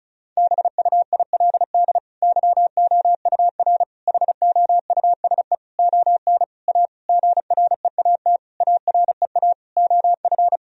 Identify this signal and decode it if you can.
Morse code, heard as a keyed tone, about 34 wpm, 700 Hz. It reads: BUILD YOUR HOUSE ON A GREAT AREA OF
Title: Sentences from Top 1000 Words - Review Part 2 - 35wpm